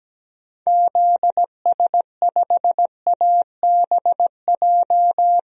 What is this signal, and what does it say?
Morse code, heard as a keyed tone, about 17 wpm, 700 Hz. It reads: ZS5ABJ
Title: DXCC-100 Call Signs - 6 Characters - 17wpm